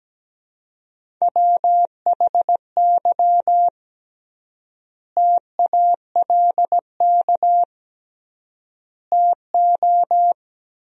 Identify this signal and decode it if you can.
Morse code, heard as a keyed tone, about 17 wpm, 700 Hz. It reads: WHY TALK TO
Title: 3x Word-Spacing - Sentences from Top 400 Words - 17wpm